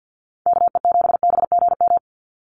Morse code, heard as a keyed tone, about 48 wpm, 700 Hz. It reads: KE7BGM
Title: NA SOTA Activator Callsigns 50wpm